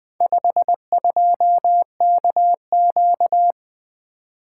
Morse code, heard as a keyed tone, about 20 wpm, 700 Hz. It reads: 52KQ
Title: Four Letter-Number Combinations - 20wpm